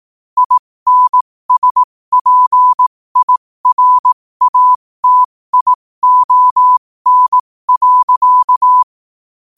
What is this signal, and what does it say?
Morse code, heard as a keyed tone, about 18 wpm, spaced as for 15 wpm, 1000 Hz. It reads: INSPIRATION.
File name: qotdfiles / 15wpm